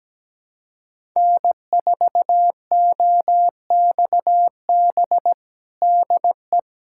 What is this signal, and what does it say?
Morse code, heard as a keyed tone, about 17 wpm, 700 Hz. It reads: N4OXB DE
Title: USA Callsign DE Callsign - 5 Characters - 17wpm